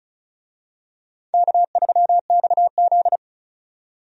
Morse code, heard as a keyed tone, about 36 wpm, 700 Hz. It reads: K3XZ
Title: USA Call Signs - 4 Characters - 35wpm